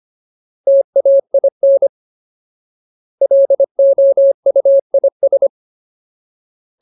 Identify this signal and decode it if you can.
Morse code, heard as a keyed tone, about 25 wpm, 550 Hz. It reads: TAIN LOUIS
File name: Ultimate Word List / 25WPM Ultimate